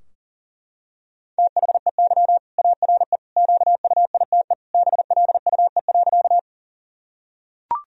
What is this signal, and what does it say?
Morse code, heard as a keyed tone, about 40 wpm, 700 Hz. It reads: THEY ARE QUITE BLUE.
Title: Sentences from Top 1000 Words - Review Part 3 - 40wpm